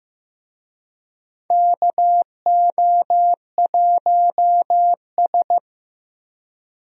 Morse code, heard as a keyed tone, about 15 wpm, 700 Hz. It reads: KO1S
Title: USA Call Signs - 4 Characters - Repeated 3x - 15wpm